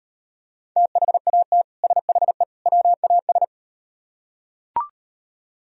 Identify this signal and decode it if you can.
Morse code, heard as a keyed tone, about 38 wpm, 700 Hz. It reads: THAT SHE WAS